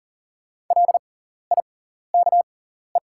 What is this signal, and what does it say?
Morse code, heard as a keyed tone, about 40 wpm, spaced as for 14 wpm, 700 Hz. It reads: LIKE
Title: 6x Character Spacing - Top 100 Words - 40wpm